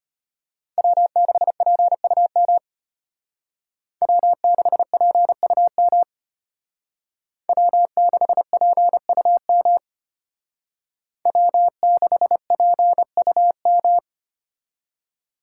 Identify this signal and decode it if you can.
Morse code, heard as a keyed tone, about 31 wpm, 700 Hz. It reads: W6PUM W6PUM W6PUM W6PUM
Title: Speed Racing - USA 1x3 Call Signs 25wpm